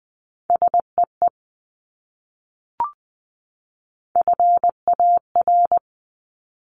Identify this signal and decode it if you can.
Morse code, heard as a keyed tone, about 20 wpm, 700 Hz. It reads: SEE FAR
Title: Top 300 Words 20wpm